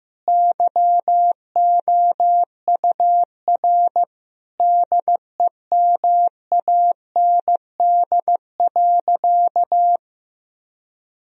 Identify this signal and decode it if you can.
Morse code, heard as a keyed tone, about 16 wpm, 700 Hz. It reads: YOUR DEMAND.
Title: Sentences from Top 1000 Words - Review Part 4 - 15wpm